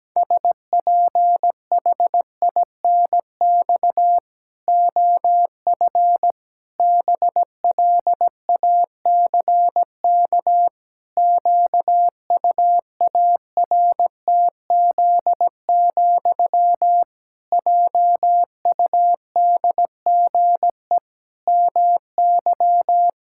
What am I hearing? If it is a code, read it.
Morse code, heard as a keyed tone, about 17 wpm, 700 Hz. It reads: SPHINX OF BLACK QUARTZ, JUDGE MY
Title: Pangram Challenge v2 - 17wpm